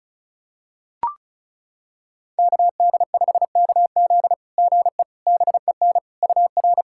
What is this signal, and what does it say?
Morse code, heard as a keyed tone, about 36 wpm, 700 Hz. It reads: KD5KZ GE BEN UR
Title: SOTA QSO Practice 35wpm